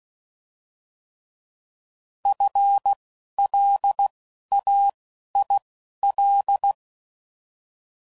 Morse code, heard as a keyed tone, about 16 wpm, spaced as for 11 wpm, 800 Hz. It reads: FLAIL
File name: Prefixes / By First Letter / Fs